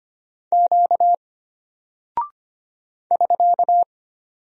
Morse code, heard as a keyed tone, about 25 wpm, 700 Hz. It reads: Q <SK>
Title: Single Character (Letter:Number:Punctuation:Prosign) Alt 25wpm